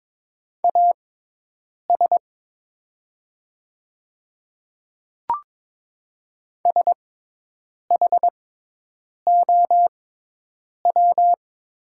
Morse code, heard as a keyed tone, about 22 wpm, spaced as for 8 wpm, 690 Hz. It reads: AS SHOW
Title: 6x Character Spacing - Top 300 Words - 22wpm